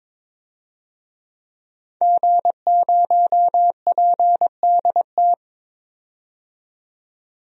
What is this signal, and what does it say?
Morse code, heard as a keyed tone, about 22 wpm, 700 Hz. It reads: G0PDT
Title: DXCC-100 Call Signs - 5 Characters - 22wpm